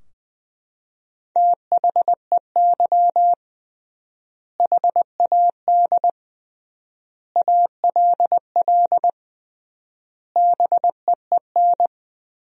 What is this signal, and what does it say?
Morse code, heard as a keyed tone, about 20 wpm, 700 Hz. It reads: THEY HAD ALL BEEN
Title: 3x Word-Spacing - Sentences from Top 200 Words - 20wpm